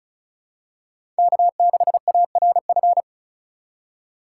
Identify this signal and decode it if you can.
Morse code, heard as a keyed tone, about 34 wpm, 700 Hz. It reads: K6ARF
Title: USA Call Signs - 5 Characters - 2x Character Spacing - 35wpm